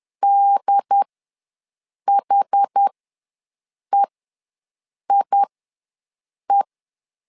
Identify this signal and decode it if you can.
Morse code, heard as a keyed tone, about 10 wpm, spaced as for 6 wpm, 800 Hz. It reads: DHEIE
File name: 2 UHDN ws 10wpm